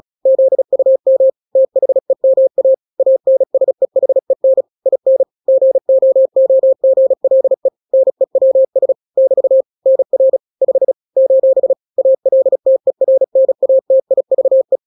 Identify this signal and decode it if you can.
Morse code, heard as a keyed tone, about 36 wpm, 530 Hz. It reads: ZUM THEMA ANSEHEN IN GOOGLE NEWS <BT> NR 5 : ALTERNATIVE
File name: daily / 35wpm